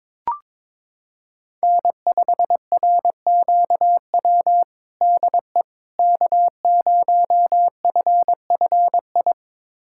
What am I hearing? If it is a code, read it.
Morse code, heard as a keyed tone, about 21 wpm, 700 Hz. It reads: N5RQW DE K0FFI
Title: USA Callsign DE Callsign 22wpm